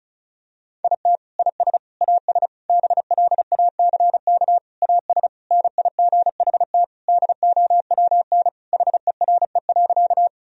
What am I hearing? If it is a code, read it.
Morse code, heard as a keyed tone, about 34 wpm, 700 Hz. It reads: IT IS AS BLACK AS NIGHT DOWN HERE.
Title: Sentences from Top 1000 Words - Review Part 2 - 35wpm